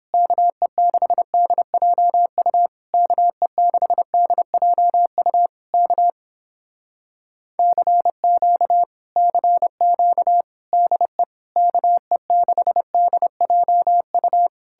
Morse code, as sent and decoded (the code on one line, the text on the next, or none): -.- . -.... -.. .--- ..- / -.- . -.... -.. .--- ..- / -.- / -.-. --.- / -.-. --.- / -.. / -.- . -.... -.. .--- ..-
KE6DJU KE6DJU K CQ CQ DE KE6DJU